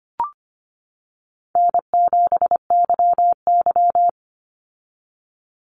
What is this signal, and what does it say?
Morse code, heard as a keyed tone, about 25 wpm, 700 Hz. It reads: N7YY
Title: Easy Call Signs - 25wpm